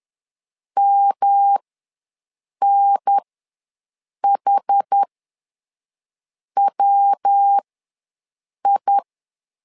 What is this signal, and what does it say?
Morse code, heard as a keyed tone, about 10 wpm, spaced as for 5 wpm, 800 Hz. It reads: MNHWI
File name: LEZIONI / 3 MWRO / 3 MW ws 10wpm 2213